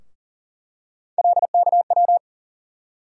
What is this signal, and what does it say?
Morse code, heard as a keyed tone, about 40 wpm, 700 Hz. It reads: LKW
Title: USA Call Sign Postfixes - 3 Letters 40wpm